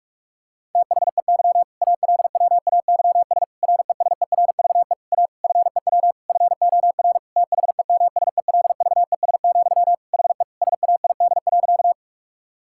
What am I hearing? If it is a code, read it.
Morse code, heard as a keyed tone, about 41 wpm, 700 Hz. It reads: THEY ALWAYS RESERVE A FEW FOR THEMSELVES, HE SAID.
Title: Sentences from Top 1000 Words - Review Part 6 - 45wpm